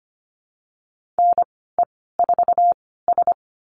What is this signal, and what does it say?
Morse code, heard as a keyed tone, about 25 wpm, spaced as for 16 wpm, 700 Hz. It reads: NE4S
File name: Lesson 15g - Number 5 - Call Signs 25wpm